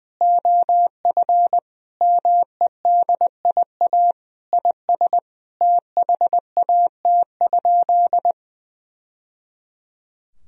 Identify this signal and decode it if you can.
Morse code, heard as a keyed tone, about 20 wpm, 700 Hz. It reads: OF MEDIA IS THAT?
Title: Sentences from Top 1000 Words - Review Part 4 - 20wpm